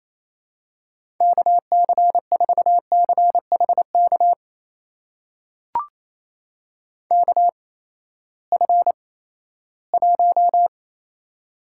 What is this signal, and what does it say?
Morse code, heard as a keyed tone, about 28 wpm, spaced as for 16 wpm, 700 Hz. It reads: KC4CHK KF1
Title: USA Call Signs - 8x Character Spacing - 28wpm